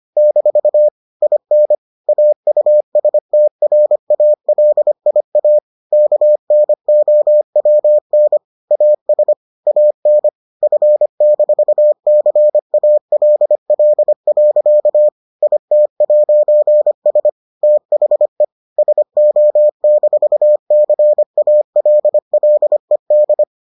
Morse code, heard as a keyed tone, about 25 wpm, 600 Hz. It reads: - IN AUSTRALIA KNOWN AS AN F-CALL. IT'S THE SO-CALLED